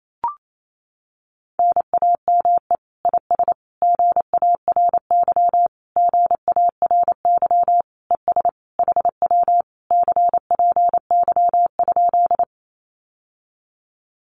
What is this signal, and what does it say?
Morse code, heard as a keyed tone, about 28 wpm, 700 Hz. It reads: NAME IS GARY GARY ES HW CPY?
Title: Common QSO Phrases Encore 28wpm